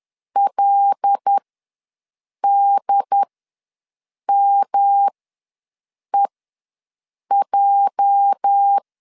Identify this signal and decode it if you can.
Morse code, heard as a keyed tone, about 10 wpm, spaced as for 6 wpm, 800 Hz. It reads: LDMEJ
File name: LEZIONI / 8 YJ45 / 8 YJ45 ws 10wpm 3843